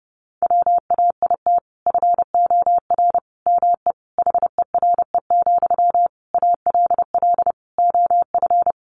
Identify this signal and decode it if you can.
Morse code, heard as a keyed tone, about 28 wpm, 700 Hz. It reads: WAIT FOR ME HERE, ALL OF
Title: Sentences from Top 500 Words 30wpm